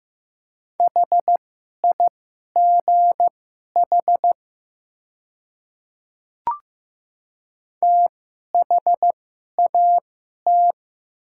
Morse code, heard as a keyed tone, about 15 wpm, spaced as for 11 wpm, 700 Hz. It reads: HIGH THAT
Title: 2x Character Spacing - Top 300 Words - 15wpm